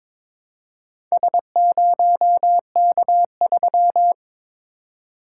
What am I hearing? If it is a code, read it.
Morse code, heard as a keyed tone, about 21 wpm, 690 Hz. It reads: S0K3
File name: practice / 22wpm / Four Letter-Number Combinations - 22wpm